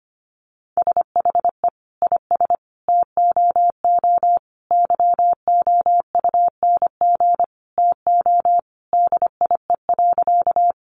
Morse code, heard as a keyed tone, about 25 wpm, 700 Hz. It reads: SHE IS TOO YOUNG TO DIE.
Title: Sentences from Top 1300 Words 25wpm